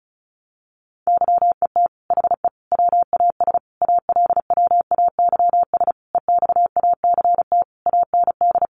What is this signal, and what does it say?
Morse code, heard as a keyed tone, about 34 wpm, 700 Hz. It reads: YET HE WAS ALWAYS EXACT AND
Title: Sentences from Top 1300 Words 35wpm